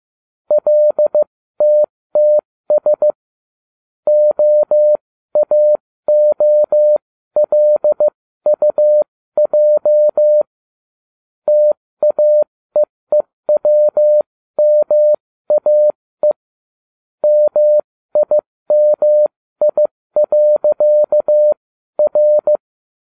Morse code, heard as a keyed tone, about 15 wpm, spaced as for 13 wpm, 600 Hz. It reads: LTTS OAOLUJ TAEEWMAE MIMI.R